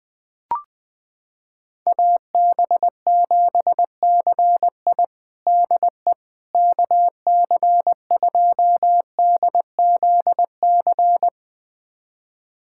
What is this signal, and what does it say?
Morse code, heard as a keyed tone, about 20 wpm, 700 Hz. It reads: AB7CI DE KC2DZC
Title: USA Callsign DE Callsign 20wpm